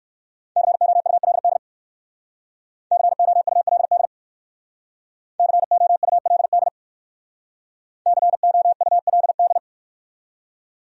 Morse code, heard as a keyed tone, about 62 wpm, 700 Hz. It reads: COULD COULD COULD COULD